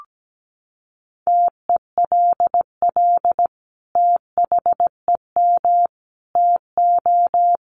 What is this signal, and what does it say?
Morse code, heard as a keyed tone, about 17 wpm, 700 Hz. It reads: TELL THEM TO
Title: Sentences from Top 1400 Words 17wpm